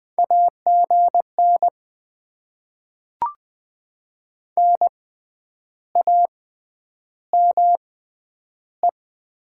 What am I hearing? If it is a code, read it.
Morse code, heard as a keyed tone, about 20 wpm, spaced as for 9 wpm, 700 Hz. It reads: AGN NAME
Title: Top 20 QSO Elements - 6x Character Spacing - 20wpm